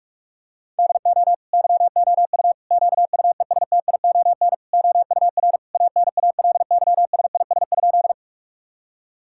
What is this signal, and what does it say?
Morse code, heard as a keyed tone, about 46 wpm, 700 Hz. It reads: DO YOU QUESTION OUR ANALYSIS?
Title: Sentences from Top 1000 Words - Review Part 5 - 45wpm